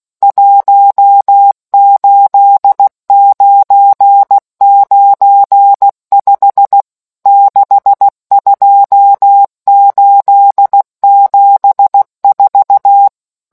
Morse code, heard as a keyed tone, about 16 wpm, 800 Hz. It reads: 18995 62874